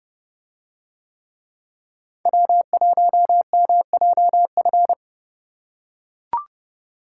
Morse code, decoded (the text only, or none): W1MJF